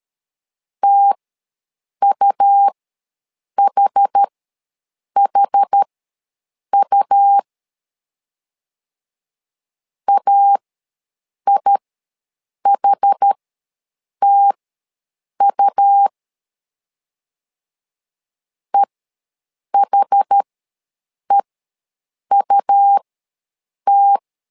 Morse code, heard as a keyed tone, about 13 wpm, spaced as for 7 wpm, 800 Hz. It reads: TUHHU AIHTU EHEUT
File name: LEZIONI / 2 UHDN / 2 UH ws 12wpm 1730